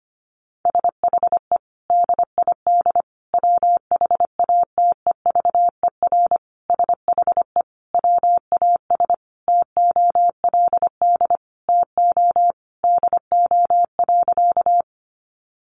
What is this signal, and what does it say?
Morse code, heard as a keyed tone, about 25 wpm, 700 Hz. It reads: SHE DID WHATEVER SHE WAS TOLD TO DO.
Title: Sentences from Top 1100 Words 25wpm